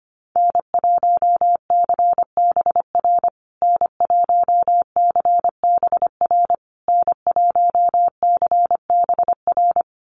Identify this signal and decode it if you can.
Morse code, heard as a keyed tone, about 25 wpm, 700 Hz. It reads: N1CBR N1CBR N1CBR
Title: USA Call Signs Repeated 3x 25wpm